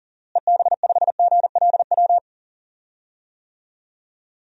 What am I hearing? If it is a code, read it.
Morse code, heard as a keyed tone, about 40 wpm, 700 Hz. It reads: EB5GLW
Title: DXCC-100 Call Signs - 6 Characters - Repeated 3x - 40wpm